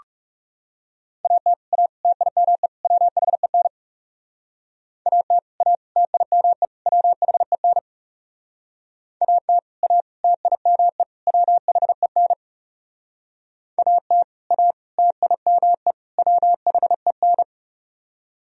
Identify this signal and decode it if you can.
Morse code, heard as a keyed tone, about 36 wpm, 700 Hz. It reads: AT A TIME WHEN AT A TIME WHEN AT A TIME WHEN AT A TIME WHEN
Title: Speed-Racing - Sets of 4 Words 30wpm